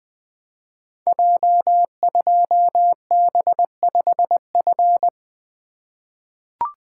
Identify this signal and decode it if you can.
Morse code, heard as a keyed tone, about 20 wpm, 700 Hz. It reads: J2B5F
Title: Code Groups - 5 Letter:Numbers - 20wpm